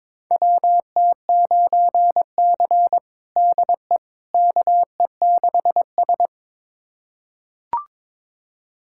Morse code, heard as a keyed tone, about 22 wpm, 700 Hz. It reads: WT9C DE KE6S
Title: USA Callsign DE Callsign - 4 Characters - 22wpm